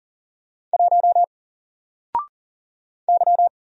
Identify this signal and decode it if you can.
Morse code, heard as a keyed tone, about 40 wpm, 700 Hz. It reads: 1 Y
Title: Single Character (Letter:Number:Punctuation:Prosign) 40wpm